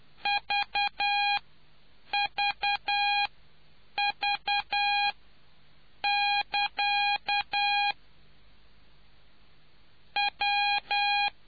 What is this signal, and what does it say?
Morse code, heard as a keyed tone, about 10 wpm, spaced as for 7 wpm, 820 Hz. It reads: VVV<KA> W